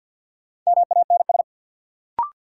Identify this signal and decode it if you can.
Morse code, heard as a keyed tone, about 48 wpm, 700 Hz. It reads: MANS
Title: Sentences from Top 1000 Words - Review Part 6 - New Word Drill - 50wpm